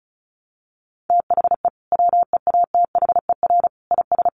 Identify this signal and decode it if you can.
Morse code, heard as a keyed tone, about 34 wpm, 700 Hz. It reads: THE WEATHER IS
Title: Sentences from Top 1200 Words 35wpm